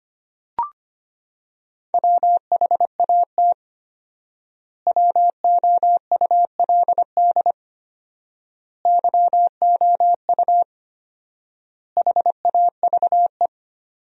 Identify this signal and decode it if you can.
Morse code, heard as a keyed tone, about 25 wpm, 700 Hz. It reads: WHAT WOULD YOU HAVE
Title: 4x Word-Spacing - Sentences from Top 300 Words - 25wpm